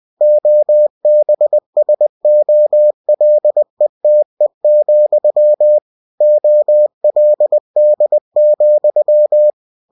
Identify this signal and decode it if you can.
Morse code, heard as a keyed tone, about 20 wpm, 600 Hz. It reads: OBSOLETE, OLD,
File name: cw-books / winnie-the-pooh / winnie-20wpm-page-0120